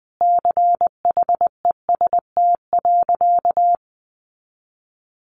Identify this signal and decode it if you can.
Morse code, heard as a keyed tone, about 20 wpm, 700 Hz. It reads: CHEST.
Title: A Sound Of Thunder 20wpm